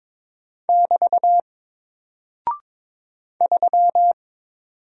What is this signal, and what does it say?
Morse code, heard as a keyed tone, about 22 wpm, 700 Hz. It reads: <BT> 3
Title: Single Character (Letter:Number:Punctuation:Prosign) Alt 22wpm